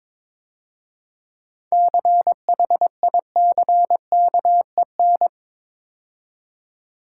Morse code, heard as a keyed tone, about 26 wpm, 690 Hz. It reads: CHICKEN
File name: Easy Words - Speed-Racing with Hints - 22wpm